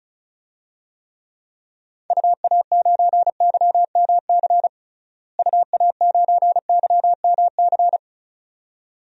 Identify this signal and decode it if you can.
Morse code, heard as a keyed tone, about 36 wpm, 700 Hz. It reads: UA9YMC UA9YMC
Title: DXCC-100 Call Signs - 6 Characters - Repeated 2x - 35wpm